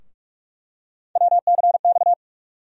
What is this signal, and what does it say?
Morse code, heard as a keyed tone, about 47 wpm, 700 Hz. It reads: WCX
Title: USA Call Sign Postfixes - 3 Letters 45wpm